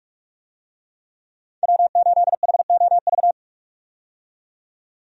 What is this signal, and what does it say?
Morse code, heard as a keyed tone, about 46 wpm, 700 Hz. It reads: W8HOV
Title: USA Call Signs - 5 Characters - 45wpm